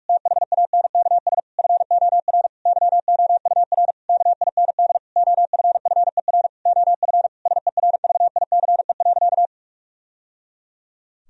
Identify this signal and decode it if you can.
Morse code, heard as a keyed tone, about 46 wpm, 700 Hz. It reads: THANKS FOR YOUR KIND OFFER OF SERVICE.
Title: Sentences from Top 1000 Words - Review Part 4 - 45wpm